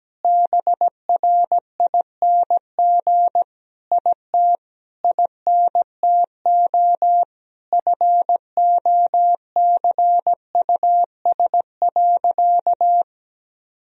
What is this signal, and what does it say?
Morse code, heard as a keyed tone, about 17 wpm, 700 Hz. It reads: BRING IT INTO FOCUS.
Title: Sentences from Top 1000 Words - Review Part 4 - 17wpm